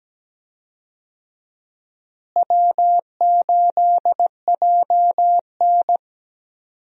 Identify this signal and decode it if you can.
Morse code, heard as a keyed tone, about 17 wpm, 700 Hz. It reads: W8JN
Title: USA Call Signs - 4 Characters - Repeated 2x - 17wpm